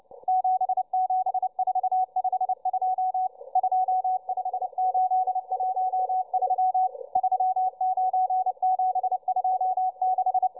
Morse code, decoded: 77452 25813 39726